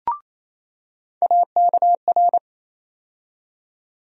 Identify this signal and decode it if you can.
Morse code, heard as a keyed tone, about 28 wpm, 700 Hz. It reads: AKR
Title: USA Call Sign Postfixes - 3 Letters 28wpm